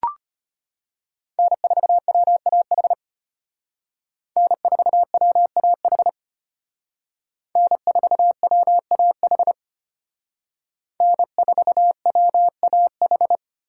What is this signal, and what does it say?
Morse code, heard as a keyed tone, about 33 wpm, 700 Hz. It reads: N4WAH N4WAH N4WAH N4WAH